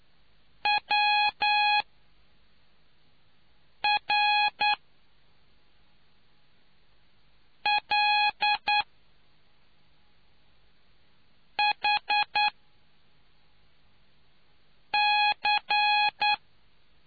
Morse code, heard as a keyed tone, about 9 wpm, spaced as for 3 wpm, 830 Hz. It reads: WRLHC